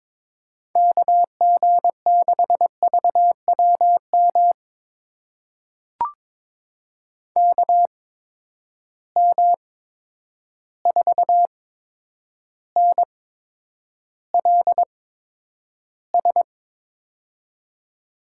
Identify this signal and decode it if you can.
Morse code, heard as a keyed tone, about 22 wpm, spaced as for 9 wpm, 700 Hz. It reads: KG6VWM KM4NLS
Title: USA Call Signs - 6 Characters - 8x Character Spacing - 22wpm